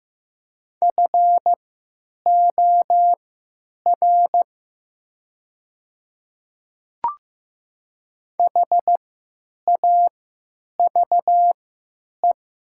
Morse code, heard as a keyed tone, about 15 wpm, spaced as for 9 wpm, 700 Hz. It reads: FOR HAVE
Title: 3x Character Spacing - Top 200 Words - 15wpm